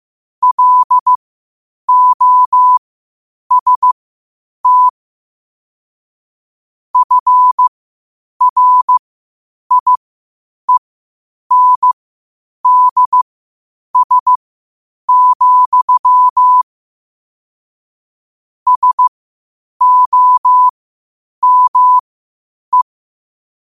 Morse code, heard as a keyed tone, about 15 wpm, spaced as for 8 wpm, 1000 Hz. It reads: LOST FRIENDS, SOME
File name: qotdfiles / 8wpm